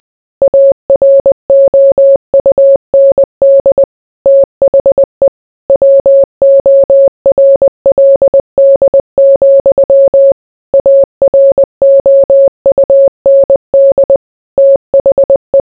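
Morse code, heard as a keyed tone, about 20 wpm, 550 Hz. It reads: AROUND THE WORLD, AROUND THE
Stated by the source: Morse code mixed with noise and music as background sound for sleep.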